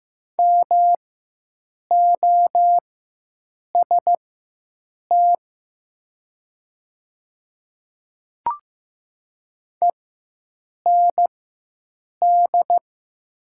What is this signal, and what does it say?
Morse code, heard as a keyed tone, about 15 wpm, spaced as for 7 wpm, 700 Hz. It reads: MOST END
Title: 4x Character Spacing - Top 300 Words - 15wpm